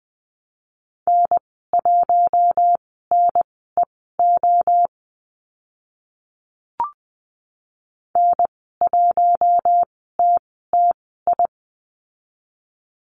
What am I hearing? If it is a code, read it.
Morse code, heard as a keyed tone, about 20 wpm, spaced as for 14 wpm, 700 Hz. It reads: N1NEO N1TTI
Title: 042 Lesson 8g - Number 1 - Call Signs 20wpm